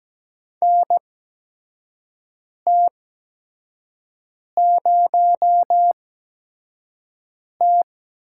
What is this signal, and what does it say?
Morse code, heard as a keyed tone, about 17 wpm, spaced as for 5 wpm, 700 Hz. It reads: NT0T
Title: USA Call Signs - 8x Character Spacing - 17wpm